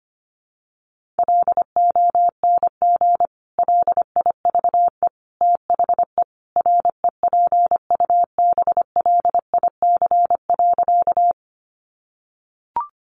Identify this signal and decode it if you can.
Morse code, heard as a keyed tone, about 25 wpm, 700 Hz. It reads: LONG LIVE THE REPUBLIC.
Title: Sentences from Top 1500 Words 25wpm